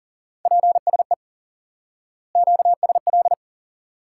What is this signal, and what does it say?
Morse code, heard as a keyed tone, about 40 wpm, 700 Hz. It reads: PSE QSL
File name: IQE Course 3 - Lesson 89 - YAGI - Two Elements - 40wpm